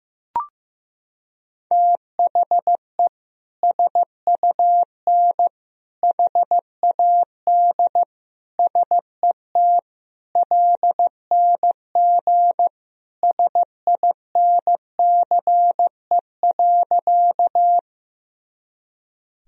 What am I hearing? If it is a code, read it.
Morse code, heard as a keyed tone, about 15 wpm, 700 Hz. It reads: THE SUN HAD SET LNG SINCE.
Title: Sentences from Top 300 Words and Abbreviations 15wpm